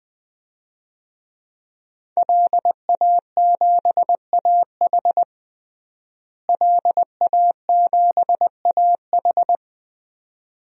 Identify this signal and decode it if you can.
Morse code, heard as a keyed tone, about 20 wpm, 700 Hz. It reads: LA7AH LA7AH
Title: DXCC-100 Call Signs - 5 Characters - Repeated 2x - 20wpm